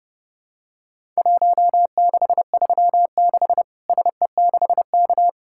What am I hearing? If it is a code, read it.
Morse code, heard as a keyed tone, about 30 wpm, 700 Hz. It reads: 1636 SE6K
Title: Worked All Europe DX Contest - QTC Exchange - 30wpm